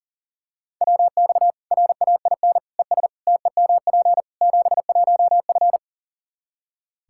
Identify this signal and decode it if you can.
Morse code, heard as a keyed tone, about 40 wpm, 700 Hz. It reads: WX RAIN ES TEMP 71F
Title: K1USN SST QSO Practice 40wpm